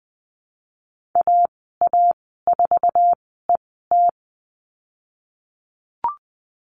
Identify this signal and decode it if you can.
Morse code, heard as a keyed tone, about 20 wpm, spaced as for 15 wpm, 700 Hz. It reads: AA4ET
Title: 154 Lesson 25g - Letter p - Call Signs 20wpm